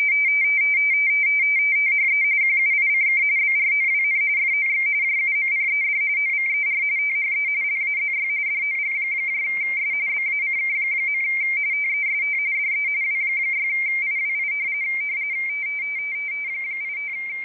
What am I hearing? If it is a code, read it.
RTTY, about 45.45 baud. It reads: THE QUICK BROWN FOX JUMPED OVER THE LAZY DOGS BACK. TEST DE W6IWI 85
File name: rtty / websdr recording 14086.0kHz